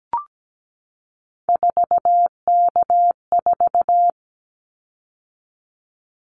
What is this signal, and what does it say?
Morse code, heard as a keyed tone, about 17 wpm, 700 Hz. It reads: 4K4
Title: Callsign Prefixes for Top 100 DXCC - 17wpm